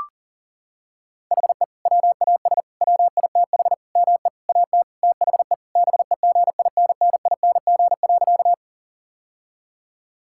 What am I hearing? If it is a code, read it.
Morse code, heard as a keyed tone, about 40 wpm, 700 Hz. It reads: HE WAS WITH ME AT THE BEGINNING.
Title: Sentences from Top 1000 Words - Review Part 5 - 40wpm